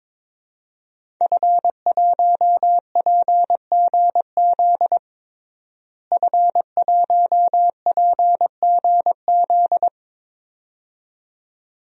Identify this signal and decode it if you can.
Morse code, heard as a keyed tone, about 21 wpm, 700 Hz. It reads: F1PGZ F1PGZ
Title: DXCC-100 Call Signs - 5 Characters - Repeated 2x - 22wpm